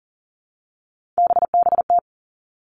Morse code, heard as a keyed tone, about 38 wpm, 700 Hz. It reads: BBT
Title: Three Challenging Letter-Number Combinations 40wpm